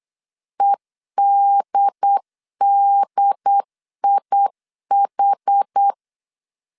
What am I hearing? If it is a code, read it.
Morse code, heard as a keyed tone, about 8 wpm, 800 Hz. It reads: EDDIH